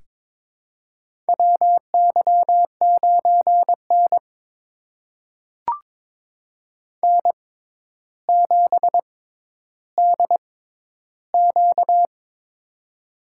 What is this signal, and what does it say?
Morse code, heard as a keyed tone, about 22 wpm, spaced as for 11 wpm, 700 Hz. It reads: WY9N N7DQ
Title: USA Call Signs - 4 Characters - 6x Character Spacing - 22wpm